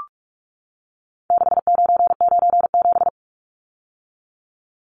Morse code, heard as a keyed tone, about 44 wpm, 700 Hz. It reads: 6997
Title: Numbers - 4-Digit 45wpm